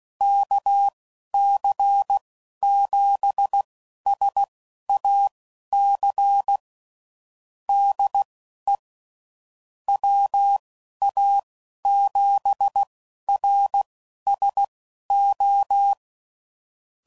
Morse code, heard as a keyed tone, about 16 wpm, spaced as for 11 wpm, 800 Hz. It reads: KC7SAC DE WA7RSO
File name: CW Code Groups @10WPM / QSOs / QSO kc7sac